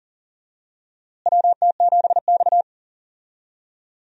Morse code, heard as a keyed tone, about 40 wpm, 700 Hz. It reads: WT7X
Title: USA Call Signs - 4 Characters - Repeated 3x - 40wpm